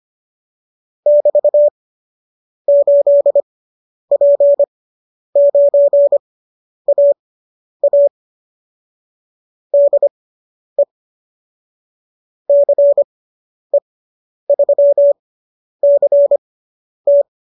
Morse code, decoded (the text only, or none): <BT> 8P9AA DE CE3CT